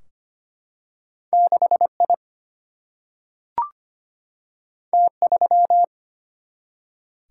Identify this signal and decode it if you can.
Morse code, heard as a keyed tone, about 25 wpm, 700 Hz. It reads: 6 I T 3
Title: Single-Letter Single-Number 25wpm